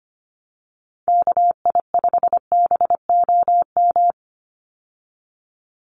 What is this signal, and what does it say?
Morse code, heard as a keyed tone, about 25 wpm, 700 Hz. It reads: KI5BOM
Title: USA Call Signs 25wpm